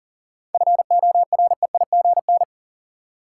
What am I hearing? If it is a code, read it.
Morse code, heard as a keyed tone, about 40 wpm, 700 Hz. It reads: FOREIGN
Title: Sentences from Top 1400 Words - New Word Drill - 40wpm